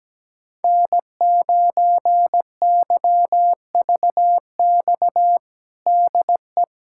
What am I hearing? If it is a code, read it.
Morse code, heard as a keyed tone, about 17 wpm, 700 Hz. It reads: N9YVX DE
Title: USA Callsign DE Callsign - 5 Characters - 17wpm